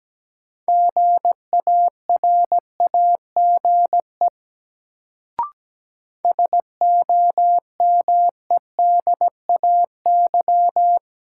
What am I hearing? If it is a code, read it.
Morse code, heard as a keyed tone, about 17 wpm, 700 Hz. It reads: GARAGE SOMEDAY
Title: Sentences from Top 1600 Words - New Word Drill Alt - 17wpm